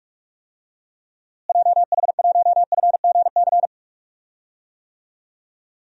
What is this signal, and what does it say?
Morse code, heard as a keyed tone, about 45 wpm, 700 Hz. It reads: JH1FGC
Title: DXCC-100 Call Signs - 6 Characters - Repeated 3x - 45wpm